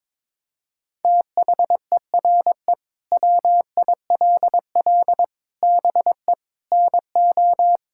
Morse code, heard as a keyed tone, about 22 wpm, 700 Hz. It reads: THERE WILL BE NO
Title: Sentences from Top 1000 Words - Review Part 5 - 22wpm